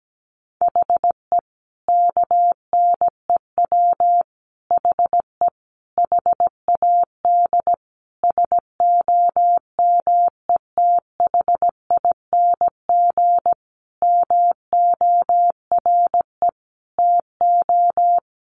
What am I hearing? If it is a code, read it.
Morse code, heard as a keyed tone, about 17 wpm, 700 Hz. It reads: HE KNEW HE HAD SOMETHING MORE TO
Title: Sentences from Top 600 Words 17wpm